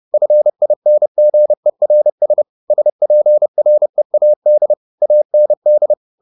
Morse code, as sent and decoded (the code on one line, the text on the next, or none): ..-. .. -. --. .-. ... / ... .--. .-. .- -.. / .- -. -..
FINGERS SPREAD AND